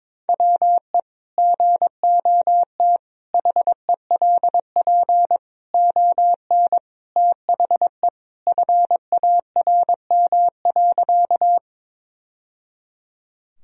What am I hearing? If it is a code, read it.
Morse code, heard as a keyed tone, about 22 wpm, 700 Hz. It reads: WE GOT HELP ON THE FARM.
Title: Sentences from Top 1000 Words - Review Part 2 - 22wpm